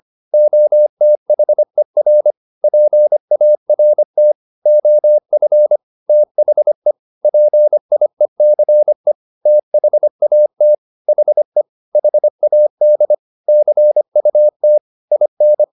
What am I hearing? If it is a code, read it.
Morse code, heard as a keyed tone, about 25 wpm, 600 Hz. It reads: OTHER PART OF THE PIECE THAT HE HAD CUT IN
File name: the old man and the sea part 1 25wpm